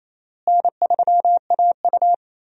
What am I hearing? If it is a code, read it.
Morse code, heard as a keyed tone, about 28 wpm, 700 Hz. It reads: N3AU
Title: USA Call Signs - 4 Characters - Repeated 3x - 28wpm